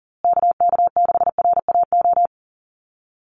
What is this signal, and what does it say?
Morse code, heard as a keyed tone, about 40 wpm, 700 Hz. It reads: KK6RAO
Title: USA Call Signs Repeated 3x 40wpm